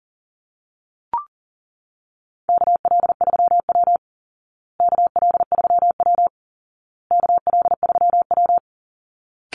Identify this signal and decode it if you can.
Morse code, heard as a keyed tone, about 40 wpm, 700 Hz. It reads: KL3W KL3W KL3W
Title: Call Signs Repeated 2x - 40wpm